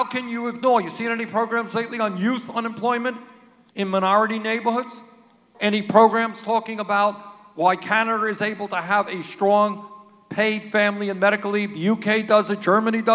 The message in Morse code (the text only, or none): none